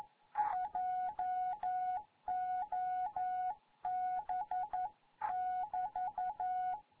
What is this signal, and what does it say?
Morse code, heard as a keyed tone, about 11 wpm, 740 Hz. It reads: JOB <BT>
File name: Morze-HA2MN